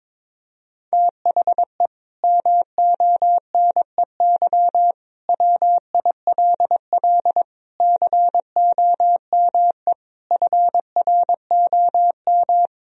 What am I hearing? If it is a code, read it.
Morse code, heard as a keyed tone, about 22 wpm, 700 Hz. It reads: THE MONEY WILL COME FROM
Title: Sentences from Top 1000 Words - Review Part 4 - 22wpm